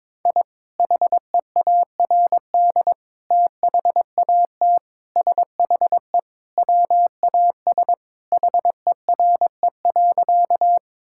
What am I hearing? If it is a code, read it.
Morse code, heard as a keyed tone, about 22 wpm, 700 Hz. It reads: I HEARD THAT SHE WAS HERE.
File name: practice / 22wpm / Sentences with 3-Word Phrases - 22wpm